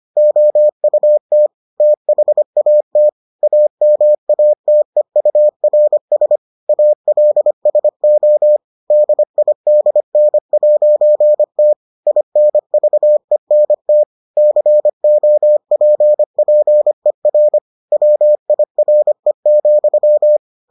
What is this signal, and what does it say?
Morse code, heard as a keyed tone, about 25 wpm, 600 Hz. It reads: OUT THAT AMATEURS ALSO DIDN'T INVENT COPPER WIRE,